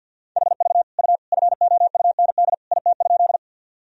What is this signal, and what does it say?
Morse code, heard as a keyed tone, about 52 wpm, 700 Hz. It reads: HV U FOUND IT?
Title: Sentences from Top 300 Words and Abbreviations 50wpm